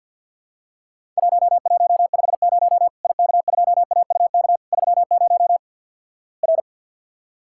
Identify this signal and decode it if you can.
Morse code, heard as a keyed tone, about 51 wpm, 700 Hz. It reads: 1150 IK2AUK 30 R
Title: Worked All Europe DX Contest - QTC Exchange - 50wpm